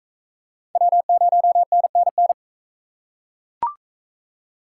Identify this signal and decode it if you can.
Morse code, heard as a keyed tone, about 40 wpm, 700 Hz. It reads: W0NNN